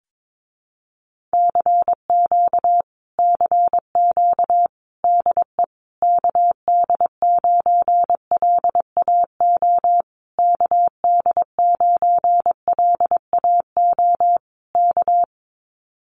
Morse code, heard as a keyed tone, about 22 wpm, 700 Hz. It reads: CQ CQ DE KD9LAO KD9LAO K
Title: Common QSO Phrases Encore 22wpm